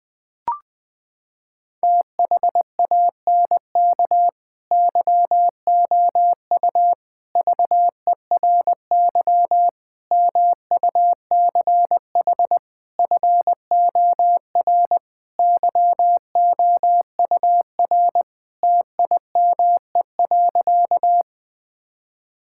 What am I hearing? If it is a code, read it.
Morse code, heard as a keyed tone, about 20 wpm, 700 Hz. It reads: THANK YOU VERY MUCH FOR YOUR TIME.
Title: Sentences with 5-Word Phrases - 20wpm